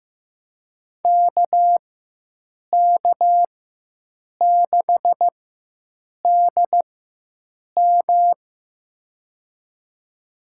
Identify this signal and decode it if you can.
Morse code, heard as a keyed tone, about 15 wpm, spaced as for 7 wpm, 700 Hz. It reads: KK6DM
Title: USA Call Signs - 4x Character Spacing - 15wpm